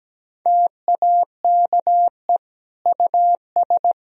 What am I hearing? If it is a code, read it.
Morse code, heard as a keyed tone, about 17 wpm, 700 Hz. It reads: TAKE US
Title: Sentences from Top 1000 Words - Review Part 4 - 17wpm